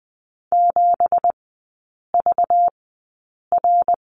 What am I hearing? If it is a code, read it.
Morse code, heard as a keyed tone, about 20 wpm, spaced as for 8 wpm, 700 Hz. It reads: 7VR
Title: 191 Lesson 31e - Letter v - Three Characters 20wpm